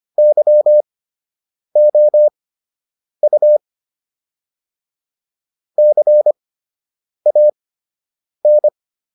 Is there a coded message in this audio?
Yes: Morse code from 0.2 to 8.7 s, heard as a keyed tone, about 25 wpm, spaced as for 8 wpm, 600 Hz. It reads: YOU CAN